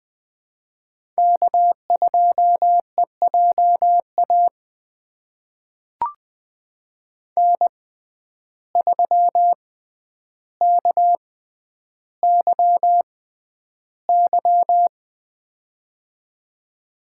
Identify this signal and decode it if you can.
Morse code, heard as a keyed tone, about 21 wpm, spaced as for 10 wpm, 700 Hz. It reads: K2EJA N3KYY
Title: USA Call Signs - 5 Characters - 6x Character Spacing - 20wpm